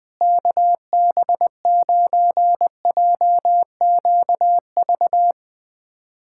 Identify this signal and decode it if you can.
Morse code, heard as a keyed tone, about 21 wpm, 700 Hz. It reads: KB9JQV
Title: USA Callsign DE Callsign 20wpm